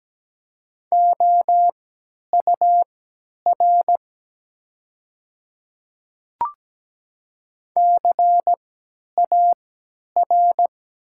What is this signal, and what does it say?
Morse code, heard as a keyed tone, about 17 wpm, spaced as for 10 wpm, 700 Hz. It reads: OUR CAR
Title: 3x Character Spacing - Top 300 Words - 17wpm